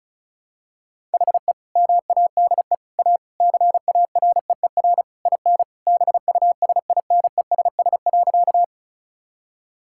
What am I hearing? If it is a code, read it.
Morse code, heard as a keyed tone, about 36 wpm, 700 Hz. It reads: HE MADE A CAREER IN BUSINESS.
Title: Sentences from Top 1000 Words - Review Part 6 - 35wpm